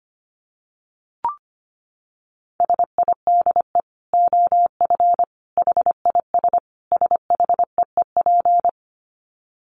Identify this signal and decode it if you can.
Morse code, heard as a keyed tone, about 25 wpm, 700 Hz. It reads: SIDE OF HIS SHEEP
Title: Dit Challenge 25wpm